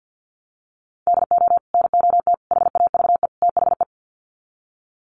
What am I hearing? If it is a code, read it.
Morse code, heard as a keyed tone, about 50 wpm, 700 Hz. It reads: DO NOT HAVE THE
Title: Sets of 4 Words 50wpm